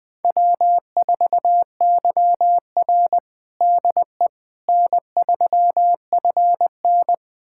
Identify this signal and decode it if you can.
Morse code, heard as a keyed tone, about 20 wpm, 700 Hz. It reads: W4YR DE N3FN
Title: USA Callsign DE Callsign - 4 Characters - 20wpm